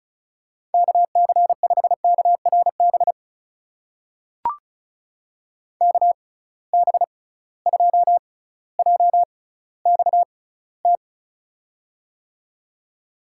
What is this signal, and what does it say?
Morse code, heard as a keyed tone, about 36 wpm, spaced as for 18 wpm, 700 Hz. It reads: KC5KRB KB2JXT
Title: USA Call Signs - 6 Characters - 6x Character Spacing - 35wpm